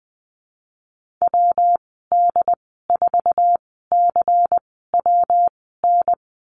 Morse code, heard as a keyed tone, about 19 wpm, spaced as for 14 wpm, 700 Hz. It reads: WD4CWN
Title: 186 Lesson 30g - Letter b - Call Signs 20wpm